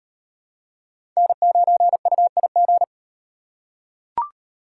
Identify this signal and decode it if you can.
Morse code, heard as a keyed tone, about 38 wpm, 700 Hz. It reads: N9UIG